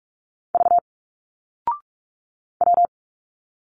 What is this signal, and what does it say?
Morse code, heard as a keyed tone, about 46 wpm, 700 Hz. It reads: V W
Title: Single Letters 45wpm